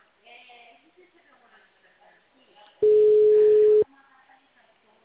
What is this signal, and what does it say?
Ringback tone (Europe, 425 Hz), with speech.
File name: out- -2044- - - .42724